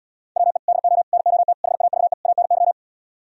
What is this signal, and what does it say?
Morse code, heard as a keyed tone, about 77 wpm, 700 Hz. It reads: WE DO NOT HAVE ANY